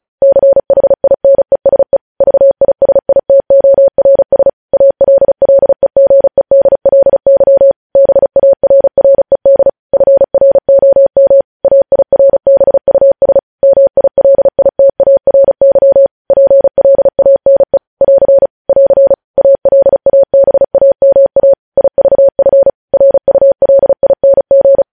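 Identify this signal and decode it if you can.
Morse code, heard as a keyed tone, about 36 wpm, 550 Hz. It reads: CHINESE VISITORS ALLEGEDLY BARRED FROM AIRBUS MILITARY PLANE <AR> <AR> ALABAMA IVF RULING
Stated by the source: Morse practice sending news headlines at 35 wpm.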